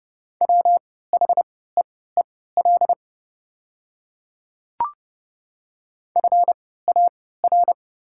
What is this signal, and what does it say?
Morse code, heard as a keyed tone, about 30 wpm, spaced as for 17 wpm, 690 Hz. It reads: WHEEL FAR
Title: 3x Character Spacing - Top 500 Words - 30wpm